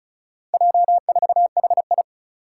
Morse code, heard as a keyed tone, about 36 wpm, 700 Hz. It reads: J4HI
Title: Four Challenging Letter-Number Combinations 35wpm